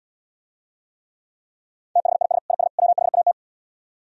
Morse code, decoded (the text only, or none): THIS IS WHAT